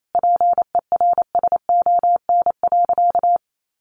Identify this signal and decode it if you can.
Morse code, heard as a keyed tone, about 28 wpm, 700 Hz. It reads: PERSON.
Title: Sentences from Top 1000 Words 28wpm